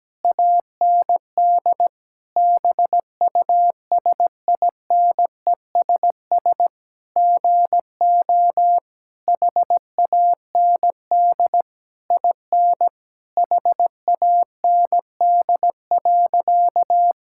Morse code, heard as a keyed tone, about 17 wpm, 700 Hz. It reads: AND BUSINESS GO HAND IN HAND.
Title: Sentences from Top 1000 Words - Review Part 5 - 17wpm